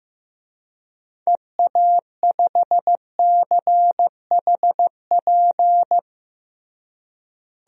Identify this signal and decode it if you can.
Morse code, heard as a keyed tone, about 15 wpm, 700 Hz. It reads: EA5CHP
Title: DXCC-100 Call Signs - 6 Characters - Repeated 2x - 15wpm